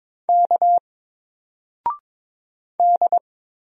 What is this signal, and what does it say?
Morse code, heard as a keyed tone, about 23 wpm, 700 Hz. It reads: K D
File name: Single-Letter-Number-22wpm